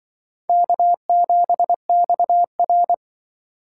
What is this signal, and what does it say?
Morse code, heard as a keyed tone, about 25 wpm, 700 Hz. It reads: K7XR
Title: Speed Racing - USA 1x2 Call Signs 20wpm